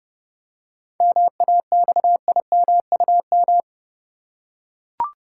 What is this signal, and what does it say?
Morse code, heard as a keyed tone, about 30 wpm, 700 Hz. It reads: MAXIMUM
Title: Seven Letter Words - 30wpm